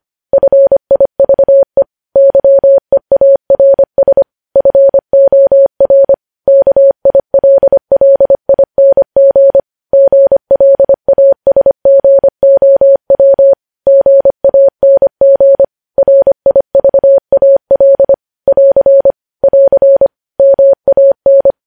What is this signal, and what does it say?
Morse code, heard as a keyed tone, about 25 wpm, 550 Hz. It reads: FIVE YEARS FOR KILLING GLASGOW GANG RIVAL <AR> <AR> MAN
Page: news headlines in morse code at 25 wpm